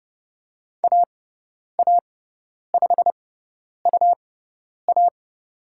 Morse code, heard as a keyed tone, about 30 wpm, spaced as for 10 wpm, 700 Hz. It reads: AA5UA
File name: Lesson 18g - Letter m - Call Signs 30wpm